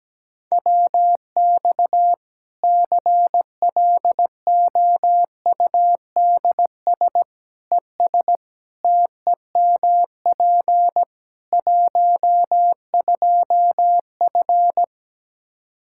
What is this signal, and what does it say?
Morse code, heard as a keyed tone, about 17 wpm, 700 Hz. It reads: WX CLOUDS ES TEMP 12F
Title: K1USN SST QSO Practice 17wpm